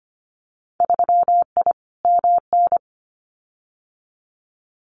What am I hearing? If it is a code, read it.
Morse code, heard as a keyed tone, about 25 wpm, 700 Hz. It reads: 3I MN
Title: Winter Field Day Exchanges 25wpm